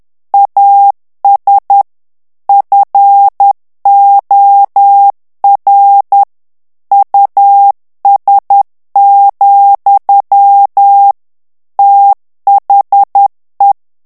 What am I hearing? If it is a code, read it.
Morse code, heard as a keyed tone, about 10 wpm, 800 Hz. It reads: AS FOR US, THE
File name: morse mp3 / ANGLAIS texte clair 2 vitesse 10 mots minute